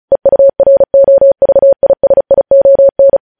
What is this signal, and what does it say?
Morse code, heard as a keyed tone, about 36 wpm, 550 Hz. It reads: EUROVISION
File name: podcast / News headlines-1x-35x35